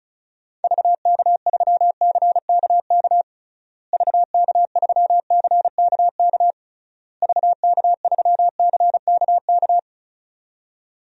Morse code, heard as a keyed tone, about 36 wpm, 700 Hz. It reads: VK3CKK VK3CKK VK3CKK
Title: DXCC-100 Call Signs - 6 Characters - Repeated 3x - 35wpm